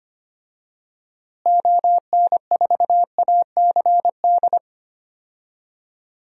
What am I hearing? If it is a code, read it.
Morse code, heard as a keyed tone, about 25 wpm, 700 Hz. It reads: ON4ACD
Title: DXCC-100 Call Signs - 6 Characters - Repeated 2x - 25wpm